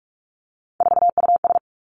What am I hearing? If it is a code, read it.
Morse code, heard as a keyed tone, about 46 wpm, 700 Hz. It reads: 4US
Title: Three Challenging Letter-Number Combinations 45wpm